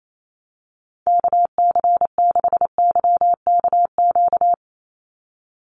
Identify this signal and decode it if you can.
Morse code, heard as a keyed tone, about 28 wpm, 700 Hz. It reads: KC6YKQ
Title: USA Call Signs 28wpm